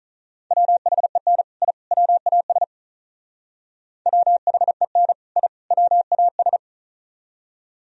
Morse code, heard as a keyed tone, about 38 wpm, 700 Hz. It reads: WHEN I WAS WHEN I WAS
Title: Speed-Racing - Sets of 3 Words 35wpm